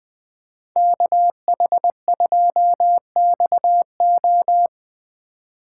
Morse code, heard as a keyed tone, about 20 wpm, 700 Hz. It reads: KH2XO